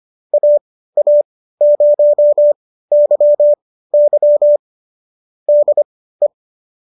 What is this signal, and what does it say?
Morse code, heard as a keyed tone, about 25 wpm, spaced as for 15 wpm, 590 Hz. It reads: AA0YY DE